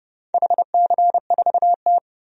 Morse code, heard as a keyed tone, about 30 wpm, 700 Hz. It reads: HC4T
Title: DXCC-100 Call Signs - 4 Characters - 30wpm